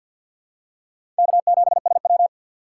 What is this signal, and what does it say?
Morse code, heard as a keyed tone, about 53 wpm, 700 Hz. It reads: K7SW
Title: USA Call Signs - 4 Characters - 6x Character Spacing - 50wpm